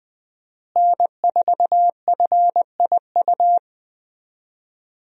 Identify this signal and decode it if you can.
Morse code, heard as a keyed tone, about 20 wpm, 700 Hz. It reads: N4FIU
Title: USA Call Signs - 5 Characters - Repeated 2x - 20wpm